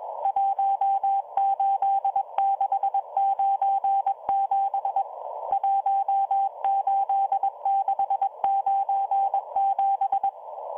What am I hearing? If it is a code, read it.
Morse code, heard as a keyed tone, about 21 wpm, 790 Hz, with radio noise. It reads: 18697 18697